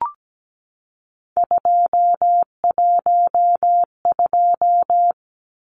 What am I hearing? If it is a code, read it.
Morse code, heard as a keyed tone, about 17 wpm, 700 Hz. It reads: 212
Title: Numbers - 3-Digit 17wpm